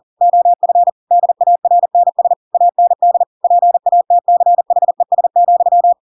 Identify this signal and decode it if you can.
Morse code, heard as a keyed tone, about 40 wpm, 700 Hz. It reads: OF DARNS AND PATCHES,
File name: Pinocchio / Pinocchio ch8 40 40 ik1hge